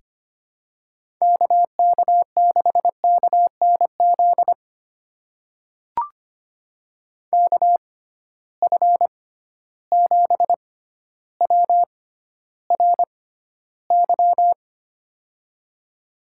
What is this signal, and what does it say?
Morse code, heard as a keyed tone, about 25 wpm, spaced as for 13 wpm, 700 Hz. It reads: KK6KNZ KF7WRY
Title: USA Call Signs - 6 Characters - 6x Character Spacing - 25wpm